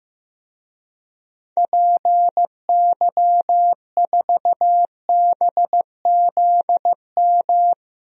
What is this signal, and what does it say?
Morse code, heard as a keyed tone, about 15 wpm, 700 Hz. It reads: PY4BZM
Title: DXCC-100 Call Signs - 6 Characters - Repeated 3x - 15wpm